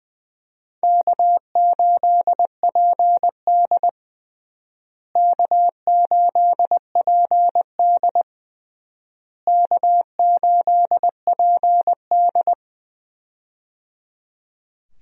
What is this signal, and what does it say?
Morse code, heard as a keyed tone, about 20 wpm, 700 Hz. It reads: K8PD K8PD K8PD
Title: USA Call Signs - 4 Characters - Repeated 3x - 20wpm